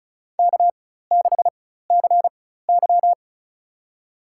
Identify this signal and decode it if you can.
Morse code, heard as a keyed tone, about 36 wpm, spaced as for 16 wpm, 700 Hz. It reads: K6CY
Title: USA Call Signs - 4 Characters - 4x Character Spacing - 35wpm